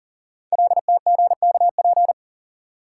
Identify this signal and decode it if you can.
Morse code, heard as a keyed tone, about 39 wpm, 700 Hz. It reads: LTGKP
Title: Code Groups - 5 Letter:Numbers - 40wpm